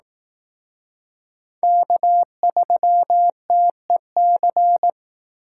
Morse code, heard as a keyed tone, about 18 wpm, 700 Hz. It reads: K3TEC
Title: Speed Racing - USA 1x3 Call Signs 15wpm